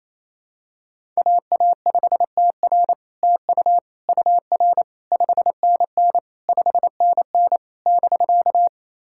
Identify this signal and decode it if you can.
Morse code, heard as a keyed tone, about 27 wpm, 700 Hz. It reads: AA5TR TU UR 5NN 5NN <BK>
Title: SOTA QSO Practice 28wpm